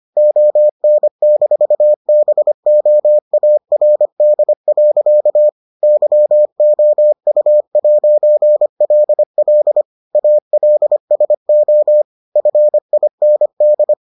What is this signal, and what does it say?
Morse code, heard as a keyed tone, about 25 wpm, 600 Hz. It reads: ON-BOARD. YOU'LL ALSO FIND